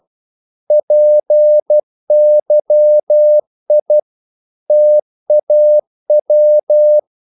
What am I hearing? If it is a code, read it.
Morse code, heard as a keyed tone, about 12 wpm, 600 Hz. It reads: PYI TAW
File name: CW / Practice / NationalCapitals12